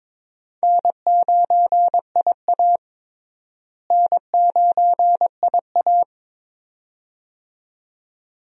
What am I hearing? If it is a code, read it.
Morse code, heard as a keyed tone, about 22 wpm, 700 Hz. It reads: N9IA N9IA
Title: USA Call Signs - 4 Characters - Repeated 2x - 22wpm